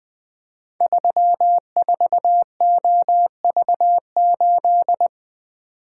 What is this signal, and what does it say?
Morse code, heard as a keyed tone, about 20 wpm, 700 Hz. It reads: 34OV8
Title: Code Groups - 5 Letter:Numbers - 20wpm